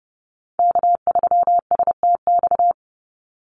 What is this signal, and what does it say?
Morse code, heard as a keyed tone, about 30 wpm, 700 Hz. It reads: K3STX
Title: Call Signs Repeated 3x - 30wpm